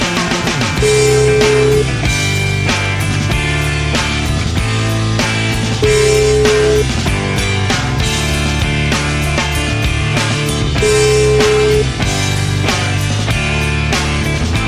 Ringback tone (Europe, 425 Hz), with noise.